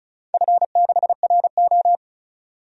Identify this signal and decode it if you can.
Morse code, heard as a keyed tone, about 36 wpm, 700 Hz. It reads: F6RO
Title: DXCC-100 Call Signs - 4 Characters - 35wpm